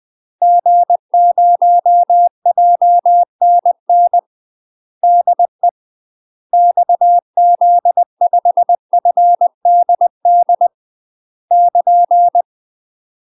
Morse code, heard as a keyed tone, about 20 wpm, 700 Hz. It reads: G0JNN DE XZ5FDD <KN>
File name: UK MASTER QSOs TRAINING / 20WPM / UK MASTER QSOs 0072